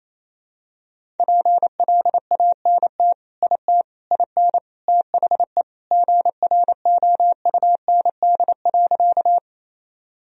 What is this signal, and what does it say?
Morse code, heard as a keyed tone, about 28 wpm, 700 Hz. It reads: PLANT IT IN THE GROUND.
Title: Sentences from Top 1000 Words - Review Part 1 - 28wpm